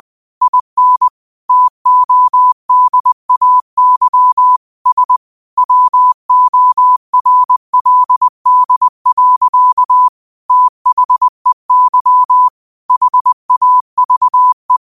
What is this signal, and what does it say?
Morse code, heard as a keyed tone, about 20 wpm, 1000 Hz. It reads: IN TODAY S WORLD. THEY HAVE
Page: Quotes for Fri, 15 Aug 2025 in Morse Code at 20 words per minute.